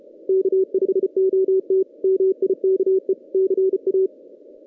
Morse code, heard as a keyed tone, about 34 wpm, 440 Hz. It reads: K5OT MIKE CA
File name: 2016 JAN CW NAQP / 2016 Jan naqp k5ot - 0-8